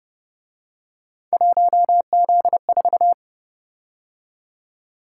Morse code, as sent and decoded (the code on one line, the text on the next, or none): .---- --.. ....-
1Z4